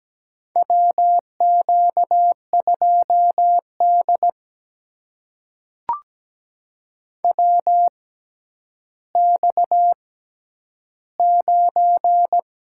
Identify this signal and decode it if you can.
Morse code, heard as a keyed tone, about 17 wpm, spaced as for 10 wpm, 700 Hz. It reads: WQ2D WX9
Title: USA Call Signs - 4 Characters - 6x Character Spacing - 17wpm